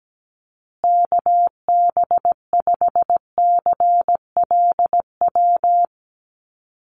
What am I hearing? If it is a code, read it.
Morse code, heard as a keyed tone, about 17 wpm, 700 Hz. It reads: KB5CLW
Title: USA Call Signs Repeated 2x 17wpm